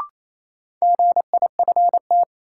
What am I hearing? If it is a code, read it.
Morse code, heard as a keyed tone, about 27 wpm, 700 Hz. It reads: GIFT
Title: Sentences from Top 900 Words - New Word Drill - 28wpm